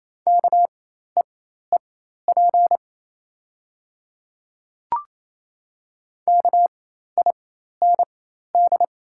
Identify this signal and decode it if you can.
Morse code, heard as a keyed tone, about 28 wpm, spaced as for 13 wpm, 700 Hz. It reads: KEEP KIND
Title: 4x Character Spacing - Top 500 Words - 28wpm